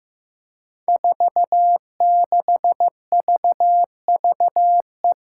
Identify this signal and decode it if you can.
Morse code, heard as a keyed tone, about 15 wpm, 700 Hz. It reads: 46VVE
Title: Five Challenging Letter-Number Combinations 15wpm